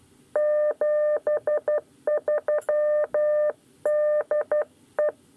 Morse code, heard as a keyed tone, about 12 wpm, 600 Hz. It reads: 73 DE
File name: 9 lezione cw 2